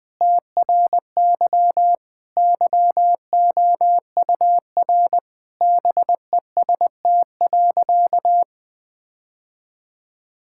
Morse code, heard as a keyed tone, about 20 wpm, 700 Hz. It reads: TRY YOUR BEST.
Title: Sentences with 7-Word Phrases - Spoken Beginning - 20wpm